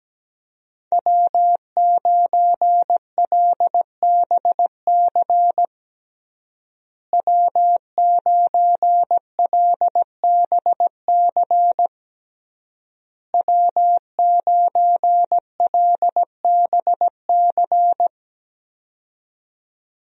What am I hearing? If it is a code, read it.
Morse code, heard as a keyed tone, about 17 wpm, 700 Hz. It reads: W9LBC W9LBC W9LBC
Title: USA Call Signs - 5 Characters - Repeated 3x - 17wpm